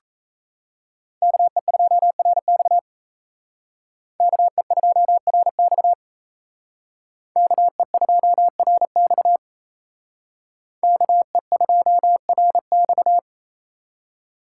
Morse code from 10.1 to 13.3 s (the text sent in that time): KE2RX